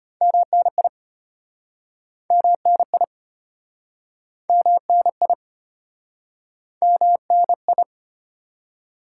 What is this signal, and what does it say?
Morse code, heard as a keyed tone, about 36 wpm, 700 Hz. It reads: MNI MNI MNI MNI